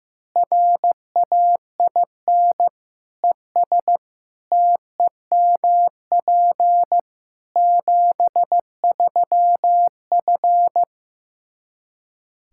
Morse code, heard as a keyed tone, about 16 wpm, 700 Hz. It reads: RAIN ES TEMP 73F
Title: Practice QSO - Exchange 2 - Info - 15wpm